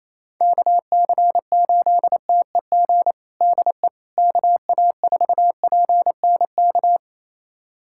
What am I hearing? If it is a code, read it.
Morse code, heard as a keyed tone, about 28 wpm, 700 Hz. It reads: KC8TEG DE KA4PNK
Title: USA Callsign DE Callsign 28wpm